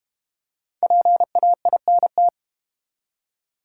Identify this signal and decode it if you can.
Morse code, heard as a keyed tone, about 34 wpm, 700 Hz. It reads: PAINT